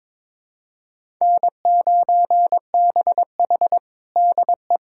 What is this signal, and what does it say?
Morse code, heard as a keyed tone, about 22 wpm, 700 Hz. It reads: N9BH DE
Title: USA Callsign DE Callsign - 4 Characters - 22wpm